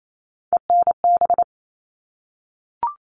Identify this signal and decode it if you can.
Morse code, heard as a keyed tone, about 28 wpm, 700 Hz. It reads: ENB
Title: All Cut Numbers - 3-Digit 28wpm